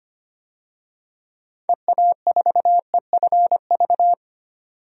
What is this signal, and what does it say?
Morse code, heard as a keyed tone, about 25 wpm, 700 Hz. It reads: EA4EFV
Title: DXCC-100 Call Signs - 6 Characters - 25wpm